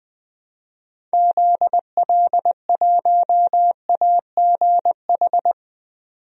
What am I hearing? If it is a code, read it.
Morse code, heard as a keyed tone, about 21 wpm, 700 Hz. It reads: ZL1AGH
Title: DXCC-100 Call Signs - 6 Characters - 20wpm